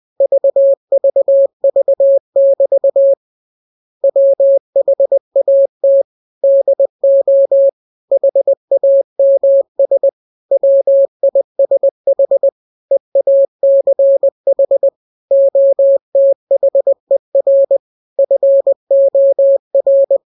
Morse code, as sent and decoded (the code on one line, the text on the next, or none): ...- ...- ...- -...- / .-- .... .- - / -.. --- / .... .- -- ... / .-- .. ... .... / . .- -.-. .... / --- - .... .-. / ..-. --- .-.
VVV<BT> WHAT DO HAMS WISH EACH OTHER FOR